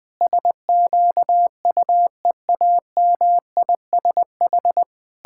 Morse code, heard as a keyed tone, about 20 wpm, 700 Hz. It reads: SQUEAMISH
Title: Pangram Challenge v2 - 20wpm